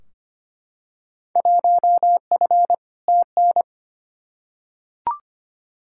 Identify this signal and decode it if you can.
Morse code, heard as a keyed tone, about 25 wpm, 700 Hz. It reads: 1F TN
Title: Field Day Exchanges - 25wpm